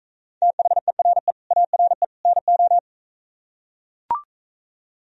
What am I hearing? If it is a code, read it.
Morse code, heard as a keyed tone, about 43 wpm, 700 Hz. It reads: THERE ARE NO